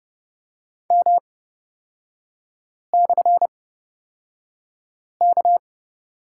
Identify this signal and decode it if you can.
Morse code, heard as a keyed tone, about 30 wpm, spaced as for 5 wpm, 700 Hz. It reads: M/K
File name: Lesson 36e - BT - Three Characters 30wpm